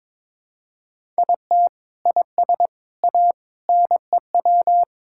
Morse code, heard as a keyed tone, about 22 wpm, 700 Hz. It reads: IT IS A NEW
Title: Sentences from Top 1000 Words - Review Part 6 - 22wpm